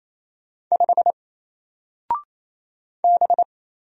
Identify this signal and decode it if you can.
Morse code, heard as a keyed tone, about 27 wpm, 700 Hz. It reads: 5 B
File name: Single-Letter-Number-28wpm